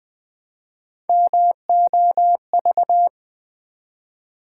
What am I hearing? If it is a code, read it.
Morse code, heard as a keyed tone, about 20 wpm, 700 Hz. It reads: MOV
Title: Build A Word - 1 Letter At A Time - 5-Letter Words - 20wpm